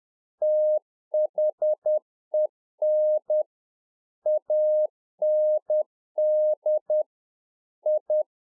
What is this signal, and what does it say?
Morse code, heard as a keyed tone, about 10 wpm, 610 Hz. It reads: THEN AND I
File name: 2011-04-mp3 / A SMALL TRANCEIVER 1 10WPM